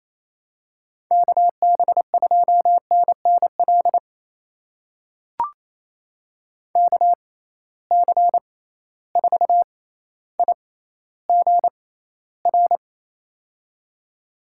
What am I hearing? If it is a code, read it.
Morse code, heard as a keyed tone, about 28 wpm, spaced as for 14 wpm, 700 Hz. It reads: KB2NNL KC4IGR
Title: USA Call Signs - 6 Characters - 6x Character Spacing - 28wpm